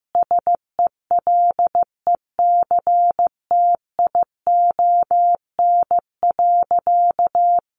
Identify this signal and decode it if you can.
Morse code, heard as a keyed tone, about 15 wpm, 700 Hz. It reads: SELECTION.
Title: Sentences from Top 1400 Words 15wpm